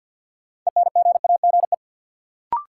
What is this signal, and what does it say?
Morse code, heard as a keyed tone, about 52 wpm, 700 Hz. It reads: ENGAGE
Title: Sentences from Top 1500 Words - New Word Drill Alt - 50wpm